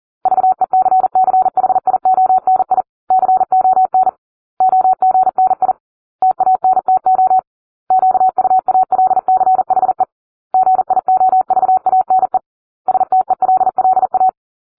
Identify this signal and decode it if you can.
Morse code, heard as a keyed tone, about 40 wpm, 750 Hz. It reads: VECCHIONI CON OGNI TANTO QUALCHE GIOVANE STELLA